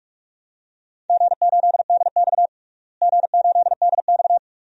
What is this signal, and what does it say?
Morse code, heard as a keyed tone, about 46 wpm, 700 Hz. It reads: G8DX G8DX
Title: DXCC-100 Call Signs - 4 Characters - Repeated 2x - 45wpm